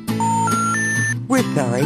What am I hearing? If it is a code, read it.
Special information tone, with speech.